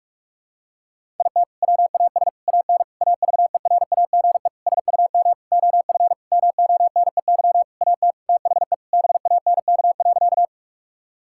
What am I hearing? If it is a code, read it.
Morse code, heard as a keyed tone, about 43 wpm, 700 Hz. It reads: IT WAS AN AVERAGE SUM OF MONEY AT THE BANK.
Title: Sentences from Top 1000 Words - Review Part 6 - 45wpm